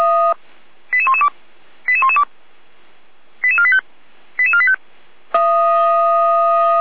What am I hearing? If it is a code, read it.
Five-tone selective call (ZVEI), heard as quick runs of five tones, tones of about 70 ms. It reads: 88112 88112 88485 88485